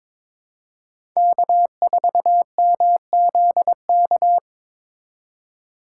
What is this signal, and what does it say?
Morse code, heard as a keyed tone, about 21 wpm, 700 Hz. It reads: K4MZK
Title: USA Call Signs - 5 Characters - 6x Character Spacing - 22wpm